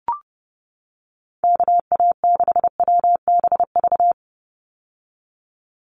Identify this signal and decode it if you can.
Morse code, heard as a keyed tone, about 28 wpm, 700 Hz. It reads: KA6WBV
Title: USA Call Signs 30wpm